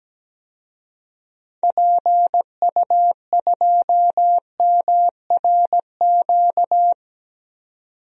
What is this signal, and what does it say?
Morse code, heard as a keyed tone, about 17 wpm, 700 Hz. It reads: PU2MRQ
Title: DXCC-100 Call Signs - 6 Characters - 17wpm